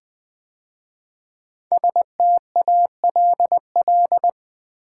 Morse code, heard as a keyed tone, about 20 wpm, 700 Hz. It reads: STALL
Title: 5 Letter Words - First 3 Letters Given - 20wpm